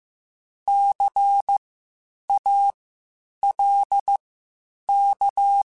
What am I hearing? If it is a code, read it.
Morse code, heard as a keyed tone, about 15 wpm, spaced as for 8 wpm, 800 Hz. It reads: CALK
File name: Spaced Alphas / Spaced Alphas mp3s / Cs sp @13